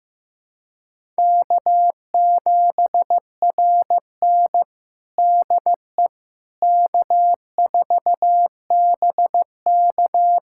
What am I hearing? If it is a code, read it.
Morse code, heard as a keyed tone, about 15 wpm, 700 Hz. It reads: K7RN DE K4BK
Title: USA Callsign DE Callsign - 4 Characters - 15wpm